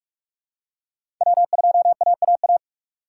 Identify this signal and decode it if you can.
Morse code, heard as a keyed tone, about 43 wpm, 700 Hz. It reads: W2AAA
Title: USA Call Signs - 4x Character Spacing - 45wpm